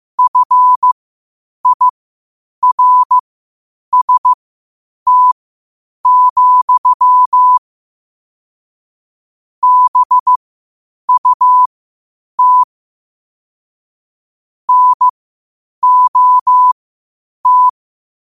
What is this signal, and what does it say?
Morse code, heard as a keyed tone, about 15 wpm, spaced as for 8 wpm, 1000 Hz. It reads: FIRST, BUT NOT
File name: 8wpm